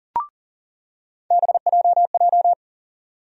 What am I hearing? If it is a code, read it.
Morse code, heard as a keyed tone, about 40 wpm, 700 Hz. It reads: BJJ
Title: USA Call Sign Postfixes - 3 Letters 40wpm